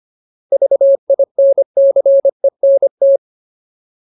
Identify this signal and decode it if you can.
Morse code, heard as a keyed tone, about 25 wpm, 550 Hz. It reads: VINCENT
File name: Ultimate Word List / 25WPM Ultimate